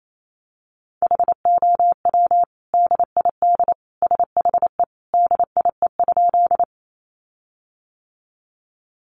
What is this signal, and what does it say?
Morse code, heard as a keyed tone, about 28 wpm, 700 Hz. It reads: HOW DID SHE DIE?
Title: Sentences from Top 1300 Words 28wpm